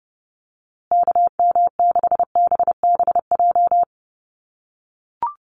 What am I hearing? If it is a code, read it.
Morse code, heard as a keyed tone, about 30 wpm, 700 Hz. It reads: KM6BBJ
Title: USA Call Signs Repeated 3x 30wpm